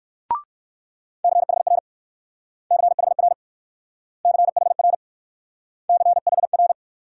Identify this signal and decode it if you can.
Morse code, heard as a keyed tone, about 60 wpm, 690 Hz. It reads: CHR CHR CHR CHR